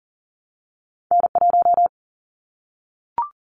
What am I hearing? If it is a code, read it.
Morse code, heard as a keyed tone, about 38 wpm, 700 Hz. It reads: N1
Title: Callsign Prefixes for Top 25 DXCC - Repeated 3x - 40wpm